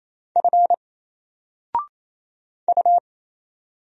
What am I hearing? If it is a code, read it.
Morse code, heard as a keyed tone, about 27 wpm, 700 Hz. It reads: F U
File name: Single-Letter-Number-28wpm